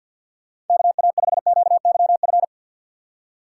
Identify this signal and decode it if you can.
Morse code, heard as a keyed tone, about 52 wpm, 700 Hz. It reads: KA5QYF
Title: USA Call Signs - 6 Characters - 3x Character Spacing - 50wpm